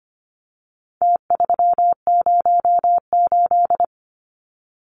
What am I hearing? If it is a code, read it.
Morse code, heard as a keyed tone, about 25 wpm, 700 Hz. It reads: T308
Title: Numbers - Contest Style - 4-Digit 25wpm